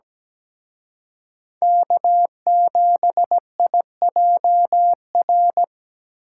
Morse code, heard as a keyed tone, about 20 wpm, 700 Hz. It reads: K7IJR
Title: Speed Racing - USA 1x3 Call Signs 17wpm